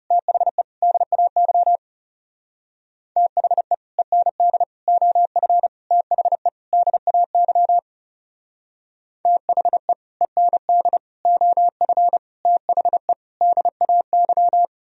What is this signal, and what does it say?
Morse code, heard as a keyed tone, about 34 wpm, 700 Hz. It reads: THE DAY THE END OF THE DAY THE END OF THE DAY
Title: Speed-Racing - Sets of 5 Words 30wpm